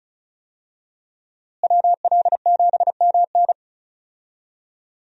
Morse code, heard as a keyed tone, about 34 wpm, 700 Hz. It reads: WL7MN
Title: USA Call Signs - 5 Characters - Repeated 3x - 35wpm